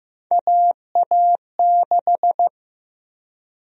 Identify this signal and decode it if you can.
Morse code, heard as a keyed tone, about 15 wpm, 700 Hz. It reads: AA6
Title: USA Call Sign Prefixes - 15wpm